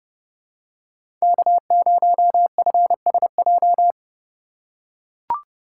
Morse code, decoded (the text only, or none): K0FSJ